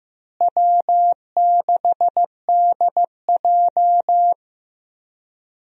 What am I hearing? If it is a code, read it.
Morse code, heard as a keyed tone, about 15 wpm, 700 Hz. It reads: W6DJ
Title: USA Call Signs - 4 Characters - Repeated 3x - 15wpm